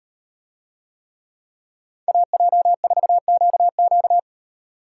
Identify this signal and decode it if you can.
Morse code, heard as a keyed tone, about 38 wpm, 700 Hz. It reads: AJ4QQ